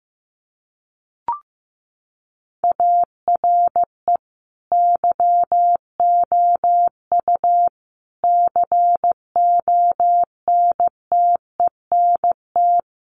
Morse code, heard as a keyed tone, about 15 wpm, 700 Hz. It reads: ARE YOU CONTENT
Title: Sentences from Top 900 Words 15wpm